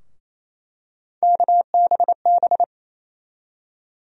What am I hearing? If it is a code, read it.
Morse code, heard as a keyed tone, about 28 wpm, 700 Hz. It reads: KBB
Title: USA Call Sign Postfixes - 3 Letters 28wpm